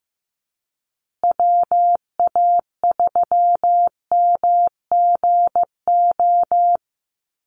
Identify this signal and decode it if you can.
Morse code, heard as a keyed tone, about 15 wpm, 700 Hz. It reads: WA3MGO
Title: USA Call Signs Repeated 3x 15wpm